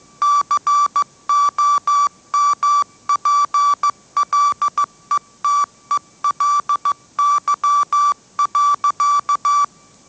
Morse code, heard as a keyed tone, about 15 wpm, 1160 Hz. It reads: COMPLETELY.
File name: mrx / mrx066 16wpm